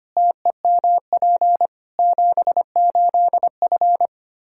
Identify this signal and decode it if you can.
Morse code, heard as a keyed tone, about 25 wpm, 700 Hz. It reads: TEMP 78F
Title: K1USN SST QSO Practice 25wpm